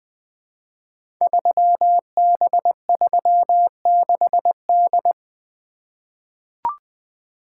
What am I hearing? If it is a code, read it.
Morse code, heard as a keyed tone, about 20 wpm, 700 Hz. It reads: 3B36D
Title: Code Groups - 5 Letter:Numbers - 20wpm